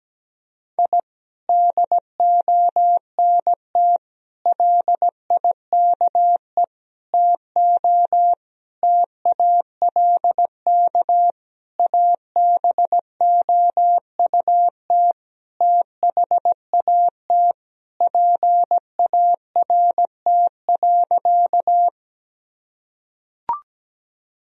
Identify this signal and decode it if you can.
Morse code, heard as a keyed tone, about 17 wpm, 700 Hz. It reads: I DONT LIKE TO TALK ABOUT THAT PART.
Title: Sentences from Top 1000 Words - Review Part 1 - 17wpm